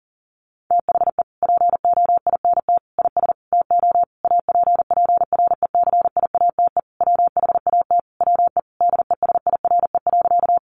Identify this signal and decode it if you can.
Morse code, heard as a keyed tone, about 40 wpm, 700 Hz. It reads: THE POINT IS TO APPRECIATE WHAT WE DESIRE.
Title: Sentences from Top 1300 Words 40wpm